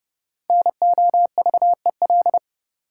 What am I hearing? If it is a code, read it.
Morse code, heard as a keyed tone, about 29 wpm, 700 Hz. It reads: NOVEL
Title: Sentences from Top 1300 Words - New Word Drill Alt - 30wpm